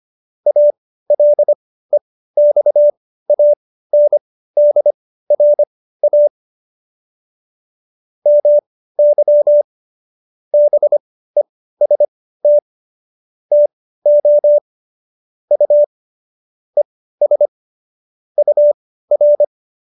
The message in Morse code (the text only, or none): ALEXANDRA MY BEST TO U ES UR